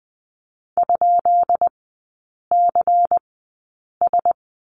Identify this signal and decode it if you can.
Morse code, heard as a keyed tone, about 20 wpm, spaced as for 8 wpm, 690 Hz. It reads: ?CS
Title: 133 Lesson 22e - Question Mark - Three Characters 20wpm